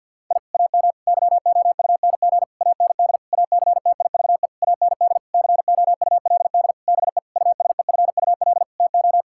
Morse code, heard as a keyed tone, about 51 wpm, 700 Hz. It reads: I AM YOUNG AND ACTIVE AND COULD BE USEFUL TO
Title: Sentences from Top 1000 Words - Review Part 6 - 50wpm